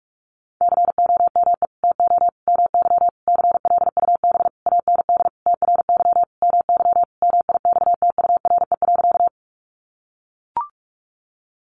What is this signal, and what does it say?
Morse code, heard as a keyed tone, about 44 wpm, 700 Hz. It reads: COME TO MY CLUB AND TRY MY MIXTURE.
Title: Sentences from Top 1500 Words 45wpm